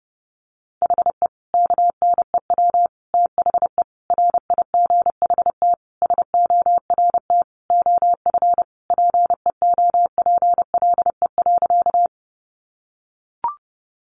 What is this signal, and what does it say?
Morse code, heard as a keyed tone, about 30 wpm, 700 Hz. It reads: HE KNEW THE RIGHT SORT OF PEOPLE.
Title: Sentences from Top 1300 Words 30wpm